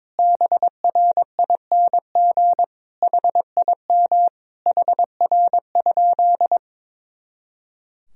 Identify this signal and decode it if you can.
Morse code, heard as a keyed tone, about 22 wpm, 700 Hz. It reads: BRING HIM HR?
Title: Sentences from Top 500 Words and Abbreviations 22wpm